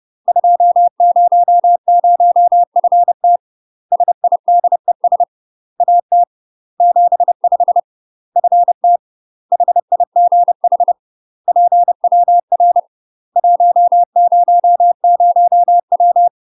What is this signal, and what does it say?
Morse code, heard as a keyed tone, about 30 wpm, 700 Hz. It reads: 200FT SIDES AT 75 FT HIGH PWR 100W
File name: UK MASTER QSOs TRAINING / 30WPM / UK MASTER QSOs 0000